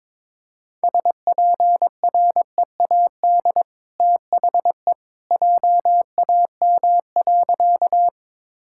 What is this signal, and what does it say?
Morse code, heard as a keyed tone, about 21 wpm, 690 Hz. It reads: SPREAD THE JAM.
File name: practice / 22wpm / Easy Sentences with Hints - 22wpm